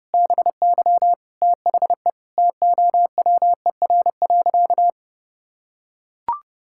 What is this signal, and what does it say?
Morse code, heard as a keyed tone, about 32 wpm, 700 Hz. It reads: BY THE TOWER.
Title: Pangram Challenge v2 - 30wpm